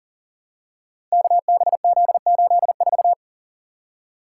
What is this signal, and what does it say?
Morse code, heard as a keyed tone, about 39 wpm, 700 Hz. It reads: KBZ84
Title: Code Groups - 5 Letter:Numbers - 40wpm